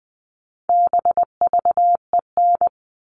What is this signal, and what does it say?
Morse code, heard as a keyed tone, about 20 wpm, 700 Hz. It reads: BVEN
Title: All Cut Numbers - 4-Digit 20wpm